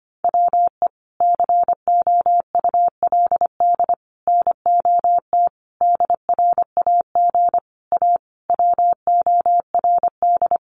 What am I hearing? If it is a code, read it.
Morse code, heard as a keyed tone, about 25 wpm, 700 Hz. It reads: WE COULD NOT DRAG A WORD
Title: Sentences from Top 1500 Words 25wpm